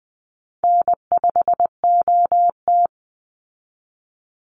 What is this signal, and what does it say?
Morse code, heard as a keyed tone, about 20 wpm, 700 Hz. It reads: N5OT
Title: Easy Call Signs Repeated 3x - 20wpm